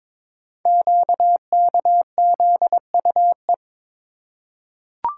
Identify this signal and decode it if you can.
Morse code, heard as a keyed tone, about 22 wpm, 700 Hz. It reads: QKZUE
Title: Code Groups - 5 Letters - 22wpm